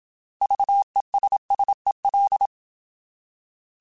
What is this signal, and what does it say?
Morse code, heard as a keyed tone, about 27 wpm, 800 Hz. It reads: VESSEL
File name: Alphabetical / Alphabetical mp3s / Vs @25